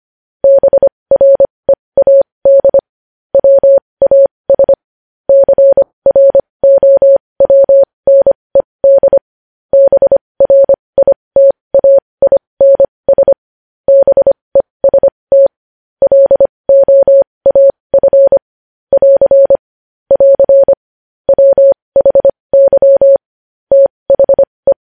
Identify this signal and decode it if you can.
Morse code, heard as a keyed tone, about 25 wpm, spaced as for 20 wpm, 550 Hz. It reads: BREAD WAS CROWNED BRITAINS BEST LOAF <AR> <AR> WHY THE
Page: news headlines in morse code at 20 wpm